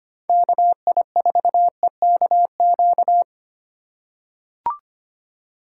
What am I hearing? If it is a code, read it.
Morse code, heard as a keyed tone, about 25 wpm, 700 Hz. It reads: KI4EKQ
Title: USA Call Signs - 6 Characters - 3x Character Spacing - 25wpm